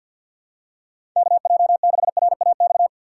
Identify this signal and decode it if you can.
Morse code, heard as a keyed tone, about 53 wpm, 700 Hz. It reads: KJ6RAX
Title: USA Call Signs - 4x Character Spacing - 50wpm